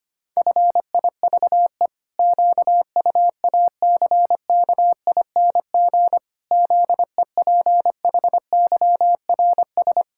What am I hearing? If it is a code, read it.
Morse code, heard as a keyed tone, about 25 wpm, 700 Hz. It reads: FIVE QUACKING ZEPHYRS
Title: Pangram Challenge v2 - 25wpm